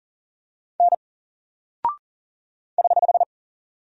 Morse code, heard as a keyed tone, about 39 wpm, 700 Hz. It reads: N <HH>
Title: Single Character (Letter:Number:Punctuation:Prosign) 40wpm